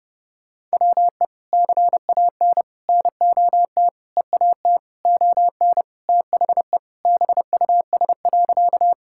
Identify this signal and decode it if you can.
Morse code, heard as a keyed tone, about 32 wpm, 700 Hz. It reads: WE CAN NOT EAT ON THE BUS.
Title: Sentences from Top 1000 Words - Review Part 6 - 30wpm